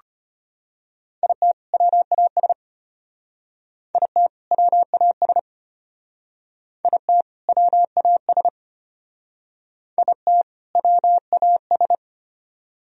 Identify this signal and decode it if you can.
Morse code, heard as a keyed tone, about 32 wpm, 700 Hz. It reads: IT WAS IT WAS IT WAS IT WAS